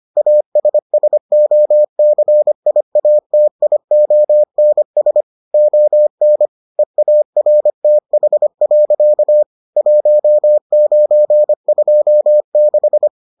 Morse code, heard as a keyed tone, about 25 wpm, 600 Hz. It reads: ASSOCIATIONS ON EARTH. 1926